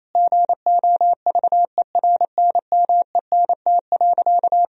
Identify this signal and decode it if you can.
Morse code, heard as a keyed tone, about 28 wpm, 700 Hz. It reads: GOVERNMENT.
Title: Sentences from Top 1000 Words - Review Part 4 - 28wpm